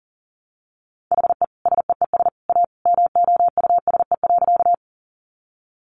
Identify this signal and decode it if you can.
Morse code, heard as a keyed tone, about 40 wpm, 700 Hz. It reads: HE SEES A MOUSE.
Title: Easy Phrases 40wpm